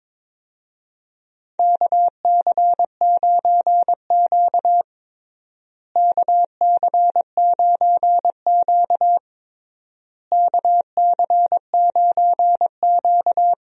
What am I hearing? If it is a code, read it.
Morse code, heard as a keyed tone, about 22 wpm, 700 Hz. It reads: KC9Q KC9Q KC9Q
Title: USA Call Signs - 4 Characters - Repeated 3x - 22wpm